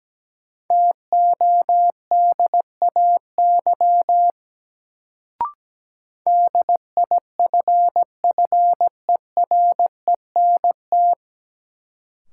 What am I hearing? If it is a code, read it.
Morse code, heard as a keyed tone, about 17 wpm, 700 Hz. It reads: TODAY DIFFERENT
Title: Sentences from Top 600 Words - New Word Drill - 17wpm